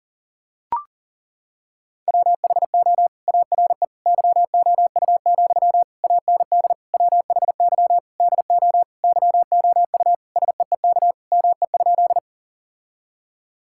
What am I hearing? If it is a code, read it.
Morse code, heard as a keyed tone, about 37 wpm, 700 Hz. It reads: WHO ARE YOU, AND WHY DO YOU SEEK ME?
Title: Sentences from Top 1600 Words 40wpm